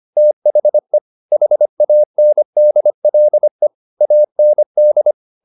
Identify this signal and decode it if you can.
Morse code, heard as a keyed tone, about 25 wpm, 600 Hz. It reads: THE HANDLE AND